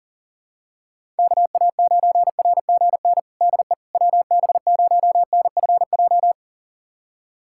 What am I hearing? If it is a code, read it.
Morse code, heard as a keyed tone, about 40 wpm, 700 Hz. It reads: KA9RGN DE WB0NFJ
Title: USA Callsign DE Callsign - 6 Characters - 40wpm